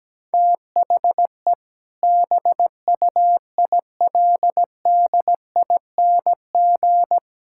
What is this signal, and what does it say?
Morse code, heard as a keyed tone, about 17 wpm, 700 Hz. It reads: THE BUILDING
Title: Sentences from Top 1000 Words - Review Part 5 - 17wpm